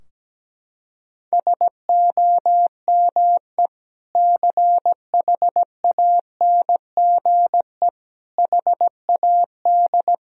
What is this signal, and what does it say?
Morse code, heard as a keyed tone, about 17 wpm, 700 Hz. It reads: SOME CHANGE HAD
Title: Sentences from Top 1000 Words - Review Part 2 - 17wpm